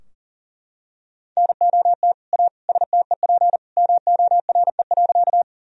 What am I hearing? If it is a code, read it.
Morse code, heard as a keyed tone, about 40 wpm, 700 Hz. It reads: NOT A STEP MORE.
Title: Sentences from Top 1000 Words - Review Part 2 - 40wpm